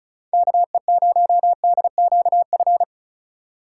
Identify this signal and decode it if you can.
Morse code, heard as a keyed tone, about 36 wpm, 690 Hz. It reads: KE0DQF
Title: USA Call Signs - 6 Characters - 3x Character Spacing - 35wpm